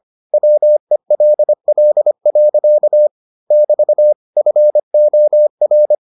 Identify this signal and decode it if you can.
Morse code, heard as a keyed tone, about 25 wpm, 600 Hz. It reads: WELL. <BT> FOR